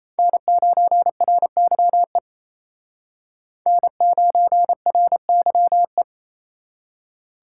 Morse code, heard as a keyed tone, about 34 wpm, 700 Hz. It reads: N9RYE N9RYE
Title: Speed Racing - USA 1x3 Call Signs 28wpm